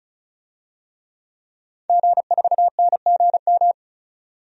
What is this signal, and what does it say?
Morse code, heard as a keyed tone, about 36 wpm, 700 Hz. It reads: G4NGM
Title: DXCC-100 Call Signs - 5 Characters - 35wpm